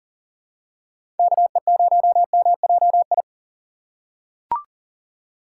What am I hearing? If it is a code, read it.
Morse code, heard as a keyed tone, about 40 wpm, 700 Hz. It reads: KE0MJI
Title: USA Call Signs - 6 Characters - 2x Character Spacing - 40wpm